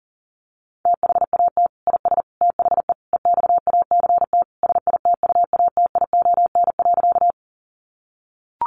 Morse code, heard as a keyed tone, about 40 wpm, 700 Hz. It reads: THAT IS THE EXACT SITUATION.
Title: Sentences from Top 1300 Words 40wpm